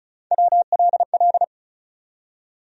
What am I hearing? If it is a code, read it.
Morse code, heard as a keyed tone, about 34 wpm, 700 Hz. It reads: WLL
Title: USA Call Sign Postfixes - 3 Letters 35wpm